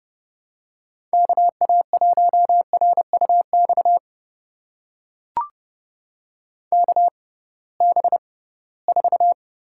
Morse code, heard as a keyed tone, about 30 wpm, spaced as for 19 wpm, 700 Hz. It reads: KA1RUX KB4
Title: USA Call Signs - 6 Characters - 6x Character Spacing - 30wpm